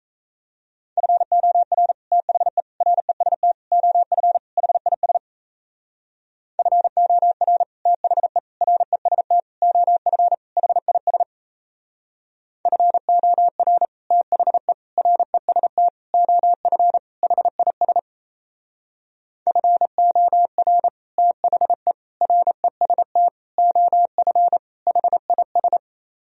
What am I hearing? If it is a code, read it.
Morse code, heard as a keyed tone, about 35 wpm, 700 Hz. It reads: FOR THE REST OF HIS FOR THE REST OF HIS FOR THE REST OF HIS FOR THE REST OF HIS